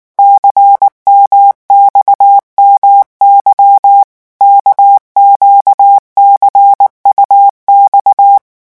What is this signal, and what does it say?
Morse code, heard as a keyed tone, about 19 wpm, 800 Hz. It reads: CMXMY KQCUX
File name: morse mp3 / revision2-vitesse 18 mots